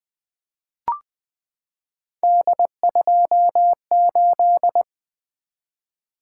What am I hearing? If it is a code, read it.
Morse code, heard as a keyed tone, about 20 wpm, 700 Hz. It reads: D28
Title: Three Letter-Number Combinations - 20wpm